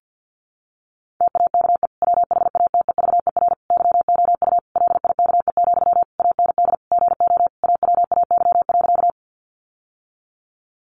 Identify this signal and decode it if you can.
Morse code, heard as a keyed tone, about 50 wpm, 700 Hz. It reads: TAKE WHATEVER YOU LIKE, AND GO AWAY.
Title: Sentences from Top 1100 Words 50wpm